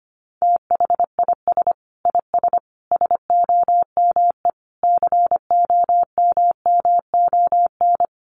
Morse code, heard as a keyed tone, about 25 wpm, 700 Hz. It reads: THIS IS SOME COMMON
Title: Sentences from Top 1400 Words 25wpm